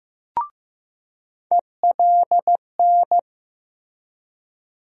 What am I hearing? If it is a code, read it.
Morse code, heard as a keyed tone, about 15 wpm, 700 Hz. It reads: ELN
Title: USA Call Sign Postfixes - 3 Letters 15wpm